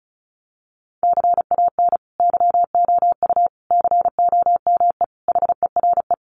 Morse code, heard as a keyed tone, about 34 wpm, 700 Hz. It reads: CAN YOU COME HERE
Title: Sentences from Top 1500 Words 35wpm